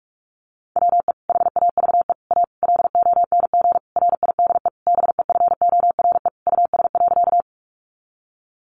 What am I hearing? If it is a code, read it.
Morse code, heard as a keyed tone, about 44 wpm, 700 Hz. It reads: WE HAVE A LONG RIDE BEFORE US.
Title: Sentences from Top 1200 Words 45wpm